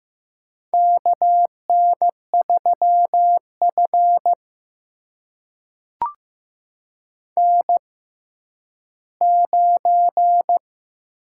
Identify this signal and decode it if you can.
Morse code, heard as a keyed tone, about 15 wpm, spaced as for 10 wpm, 700 Hz. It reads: KN3F N9
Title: USA Call Signs - 4 Characters - 6x Character Spacing - 15wpm